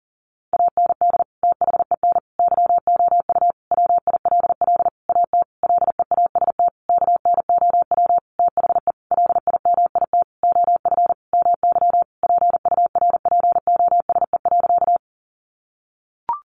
Morse code, heard as a keyed tone, about 40 wpm, 700 Hz. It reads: AND THEN YOU WILL AT LEAST KNOW THE LIMIT OF MY PURPOSE.
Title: Sentences from Top 1500 Words 40wpm